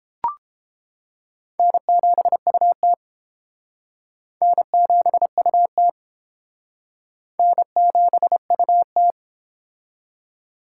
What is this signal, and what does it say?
Morse code, heard as a keyed tone, about 31 wpm, 700 Hz. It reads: N7UT N7UT N7UT